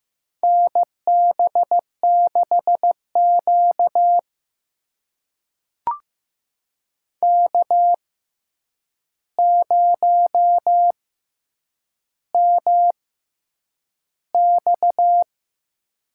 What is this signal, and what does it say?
Morse code, heard as a keyed tone, about 15 wpm, spaced as for 8 wpm, 700 Hz. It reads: NB6Q K0MX
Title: USA Call Signs - 4 Characters - 6x Character Spacing - 15wpm